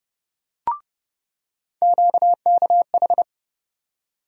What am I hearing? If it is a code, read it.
Morse code, heard as a keyed tone, about 30 wpm, 700 Hz. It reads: QKH
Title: USA Call Sign Postfixes - 3 Letters 30wpm